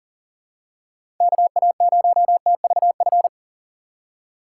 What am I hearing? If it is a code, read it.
Morse code, heard as a keyed tone, about 40 wpm, 700 Hz. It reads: KA0TVF
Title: USA Call Signs - 6 Characters - 2x Character Spacing - 40wpm